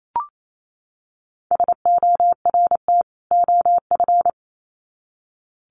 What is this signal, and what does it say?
Morse code, heard as a keyed tone, about 28 wpm, 700 Hz. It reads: SORT OF
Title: Sets of 2 Words 28wpm